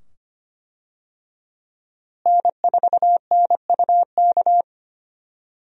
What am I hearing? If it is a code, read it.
Morse code, heard as a keyed tone, about 25 wpm, 700 Hz. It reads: N4NUK
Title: USA Call Signs - 5 Characters - Repeated 2x - 25wpm